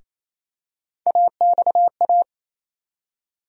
Morse code, heard as a keyed tone, about 28 wpm, 700 Hz. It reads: AXA
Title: USA Call Sign Postfixes - 3 Letters 28wpm